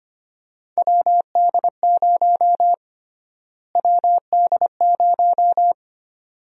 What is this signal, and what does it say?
Morse code, heard as a keyed tone, about 25 wpm, 700 Hz. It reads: WD0 WD0
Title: USA Call Sign Prefixes - Repeated 2x - 25wpm